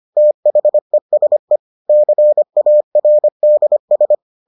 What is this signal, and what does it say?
Morse code, heard as a keyed tone, about 25 wpm, 600 Hz. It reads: THESE CARDS